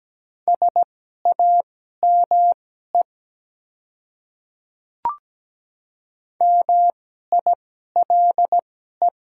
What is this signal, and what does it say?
Morse code, heard as a keyed tone, about 17 wpm, spaced as for 12 wpm, 700 Hz. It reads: SAME MILE
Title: 2x Character Spacing - Top 300 Words - 17wpm